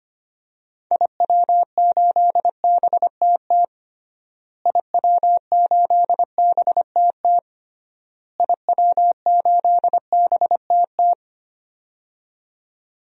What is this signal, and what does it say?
Morse code, heard as a keyed tone, about 25 wpm, 700 Hz. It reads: IW8BTT IW8BTT IW8BTT
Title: DXCC-100 Call Signs - 6 Characters - Repeated 3x - 25wpm